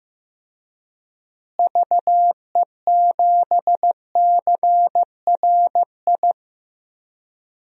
Morse code, heard as a keyed tone, about 15 wpm, 700 Hz. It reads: VE7CRI
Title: DXCC-100 Call Signs - 6 Characters - Repeated 3x - 15wpm